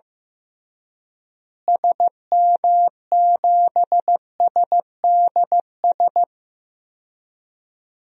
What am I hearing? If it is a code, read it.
Morse code, heard as a keyed tone, about 15 wpm, 700 Hz. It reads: SM7SDS
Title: DXCC-100 Call Signs - 6 Characters - Repeated 3x - 15wpm